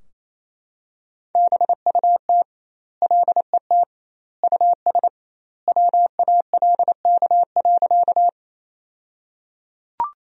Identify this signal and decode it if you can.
Morse code, heard as a keyed tone, about 27 wpm, 700 Hz. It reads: BUT LET US WALK.
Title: 2x Word-Spacing - Sentences from Top 300 Words - 28wpm